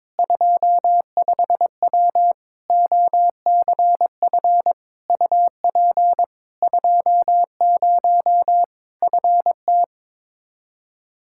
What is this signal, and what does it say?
Morse code, heard as a keyed tone, about 22 wpm, 700 Hz. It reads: 25W OCF UP 20 FT
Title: K1USN SST QSO Practice 22wpm